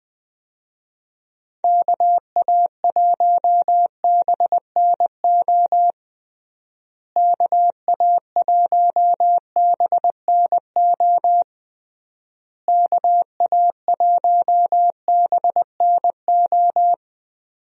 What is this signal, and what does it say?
Morse code, heard as a keyed tone, about 21 wpm, 700 Hz. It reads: KA1BNO KA1BNO KA1BNO
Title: USA Call Signs - 6 Characters - Repeated 3x - 20wpm